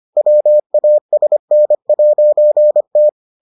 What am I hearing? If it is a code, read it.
Morse code, heard as a keyed tone, about 25 wpm, 600 Hz. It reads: WASN'T